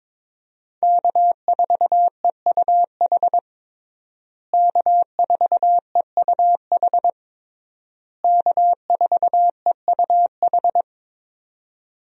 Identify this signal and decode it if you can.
Morse code, heard as a keyed tone, about 22 wpm, 700 Hz. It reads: K4EUH K4EUH K4EUH
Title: USA Call Signs - 5 Characters - Repeated 3x - 22wpm